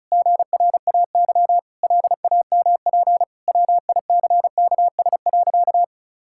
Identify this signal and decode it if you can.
Morse code, heard as a keyed tone, about 36 wpm, 700 Hz. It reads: GRAY LAMP WICKS.
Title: Pangram Challenge v2 - 35wpm